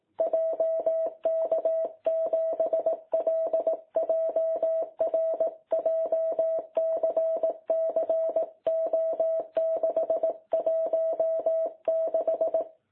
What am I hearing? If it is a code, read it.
Morse code, heard as a keyed tone, about 18 wpm, 660 Hz. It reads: JK7LJR JCC O616
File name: jh8cbh / 2021cw